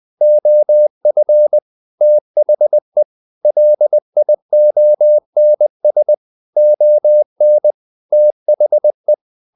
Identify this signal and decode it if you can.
Morse code, heard as a keyed tone, about 20 wpm, 600 Hz. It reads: OF THE LIONS ON THE
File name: the old man and the sea / the old man and the sea part 1 20wpm